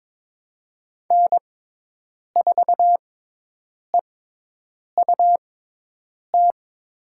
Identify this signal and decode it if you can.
Morse code, heard as a keyed tone, about 22 wpm, spaced as for 8 wpm, 700 Hz. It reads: N4EUT
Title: USA Call Signs - 5 Characters - 6x Character Spacing - 22wpm